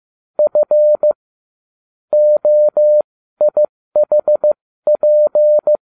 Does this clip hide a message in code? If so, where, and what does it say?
Yes: Morse code from 0.4 to 5.8 s, heard as a keyed tone, about 15 wpm, spaced as for 13 wpm, 600 Hz. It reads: F OIHP